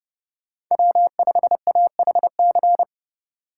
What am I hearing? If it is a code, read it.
Morse code, heard as a keyed tone, about 29 wpm, 700 Hz. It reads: W5AHC